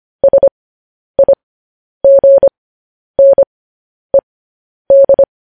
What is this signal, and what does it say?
Morse code, heard as a keyed tone, about 25 wpm, spaced as for 10 wpm, 550 Hz. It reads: SIGNED
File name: podcast / News headlines-1x-25x10